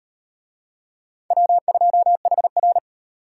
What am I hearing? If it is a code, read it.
Morse code, heard as a keyed tone, about 35 wpm, 700 Hz. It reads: W2HR